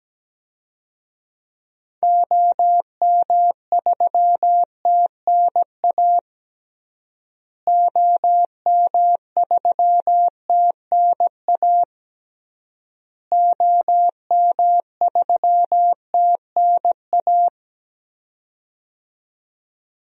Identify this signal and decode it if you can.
Morse code, heard as a keyed tone, about 17 wpm, 700 Hz. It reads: OM3TNA OM3TNA OM3TNA
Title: DXCC-100 Call Signs - 6 Characters - Repeated 3x - 17wpm